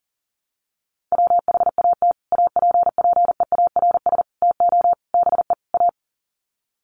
Morse code, heard as a keyed tone, about 37 wpm, 700 Hz. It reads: WHAT APPEARS TO BE A
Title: Sets of 5 Words Encore2 40wpm